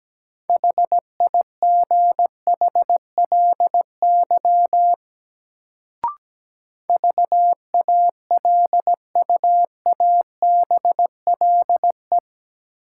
Morse code, heard as a keyed tone, about 17 wpm, 700 Hz. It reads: HIGHLY VALUABLE
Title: Sentences from Top 1100 Words - New Word Drill Alt - 17wpm